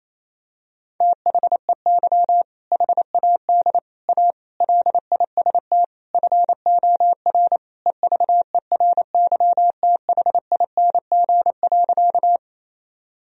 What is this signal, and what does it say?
Morse code, heard as a keyed tone, about 28 wpm, 700 Hz. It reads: THEY HAD A LIST FOR EVERYTHING.
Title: Sentences from Top 1000 Words - Review Part 2 - 28wpm